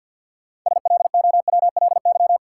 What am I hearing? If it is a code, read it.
Morse code, heard as a keyed tone, about 50 wpm, 700 Hz. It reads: SLOWLY
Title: Sentences from Top 1100 Words - New Word Drill Alt - 50wpm